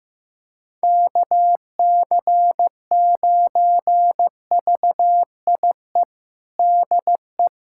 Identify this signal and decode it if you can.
Morse code, heard as a keyed tone, about 15 wpm, 700 Hz. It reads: KC9VIE DE
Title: USA Callsign DE Callsign - 6 Characters - 15wpm